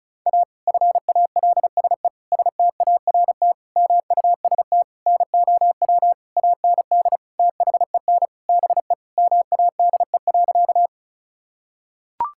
Morse code, heard as a keyed tone, about 34 wpm, 700 Hz. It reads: A FALSE START MUST NOW AND THEN BE MADE.
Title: Sentences from Top 1600 Words 35wpm